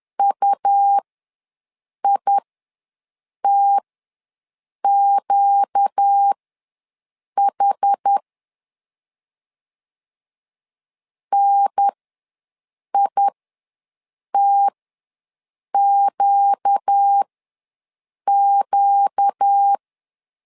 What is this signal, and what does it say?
Morse code, heard as a keyed tone, about 10 wpm, spaced as for 6 wpm, 800 Hz. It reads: UITQH NITQQ
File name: LEZIONI / 4 Q ws 10wpm 2306